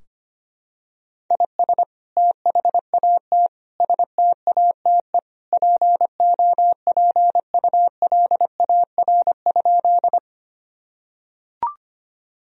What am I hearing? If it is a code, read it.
Morse code, heard as a keyed tone, about 25 wpm, 700 Hz. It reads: IS THAT STATE POPULAR?
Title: Sentences from Top 1000 Words - Review Part 3 - 25wpm